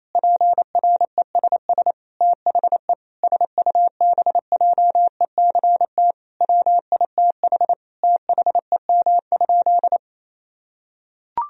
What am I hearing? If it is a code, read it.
Morse code, heard as a keyed tone, about 28 wpm, 700 Hz. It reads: PRESS THE SUBJECT WITH THEM?
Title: Sentences from Top 1000 Words - Review Part 1 - 28wpm